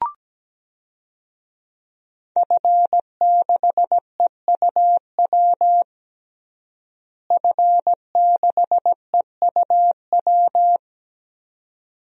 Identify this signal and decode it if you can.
Morse code, heard as a keyed tone, about 17 wpm, 700 Hz. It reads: F6EUW F6EUW
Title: DXCC-100 Call Signs - 5 Characters - Repeated 2x - 17wpm